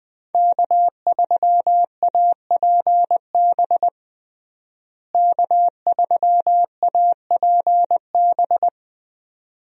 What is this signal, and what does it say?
Morse code, heard as a keyed tone, about 20 wpm, 700 Hz. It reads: K3APB K3APB
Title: USA Call Signs - 5 Characters - Repeated 2x - 20wpm